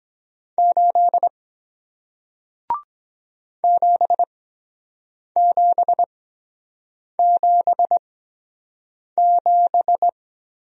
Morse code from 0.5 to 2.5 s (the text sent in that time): 8